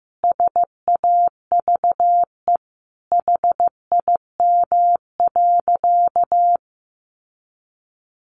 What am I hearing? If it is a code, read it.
Morse code, heard as a keyed tone, about 15 wpm, 700 Hz. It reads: SAVE HIM.
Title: Sentences from Top 1100 Words 15wpm